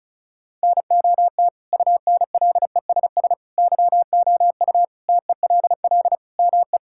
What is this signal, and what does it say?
Morse code, heard as a keyed tone, about 34 wpm, 700 Hz. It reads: NOT UNLESS YOU TELL ME
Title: Sentences from Top 1000 Words - Review Part 6 - 35wpm